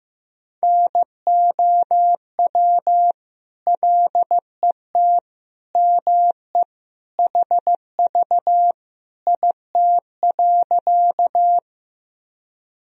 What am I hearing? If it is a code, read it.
Morse code, heard as a keyed tone, about 15 wpm, 700 Hz. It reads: NOW LET ME HV IT.
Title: Sentences from Top 300 Words and Abbreviations 15wpm